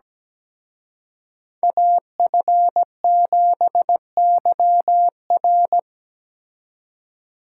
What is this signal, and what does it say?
Morse code, heard as a keyed tone, about 20 wpm, 700 Hz. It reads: AF7YR
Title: Speed Racing - USA Call Signs 17wpm